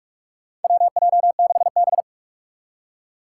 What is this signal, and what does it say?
Morse code, heard as a keyed tone, about 46 wpm, 700 Hz. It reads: WJ6B
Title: Four Challenging Letter-Number Combinations 45wpm